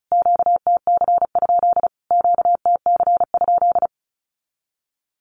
Morse code, heard as a keyed tone, about 34 wpm, 700 Hz. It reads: QTC? QTC?
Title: Q-Codes Repeated 3x 35wpm